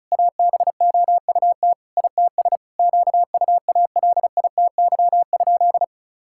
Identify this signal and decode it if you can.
Morse code, heard as a keyed tone, about 36 wpm, 700 Hz. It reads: ABOUT ITS QUALITY?
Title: Sentences from Top 1000 Words - Review Part 6 - 35wpm